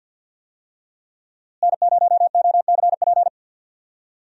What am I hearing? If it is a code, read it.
Morse code, heard as a keyed tone, about 53 wpm, 700 Hz. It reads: N0OCP
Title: USA Call Signs - 5 Characters - Repeated 3x - 50wpm